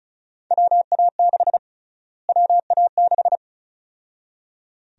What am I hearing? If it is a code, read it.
Morse code, heard as a keyed tone, about 36 wpm, 700 Hz. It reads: WA6 WA6
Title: USA Call Sign Prefixes - Repeated 2x - 35wpm